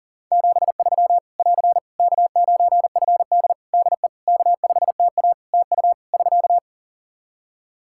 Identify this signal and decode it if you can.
Morse code, heard as a keyed tone, about 40 wpm, 700 Hz. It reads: 73 <AR> K9FD DE K5TA TU <SK>
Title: K1USN SST QSO Practice 40wpm